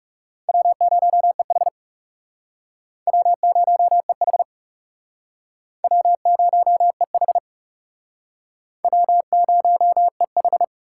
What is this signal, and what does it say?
Morse code, heard as a keyed tone, about 36 wpm, 700 Hz. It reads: W0EH W0EH W0EH W0EH
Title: Speed Racing - USA 1x2 Call Signs 30wpm